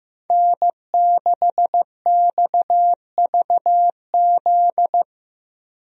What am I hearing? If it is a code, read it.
Morse code, heard as a keyed tone, about 15 wpm, 700 Hz. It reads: N6XVZ
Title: USA Call Signs - 5 Characters - Repeated 3x - 15wpm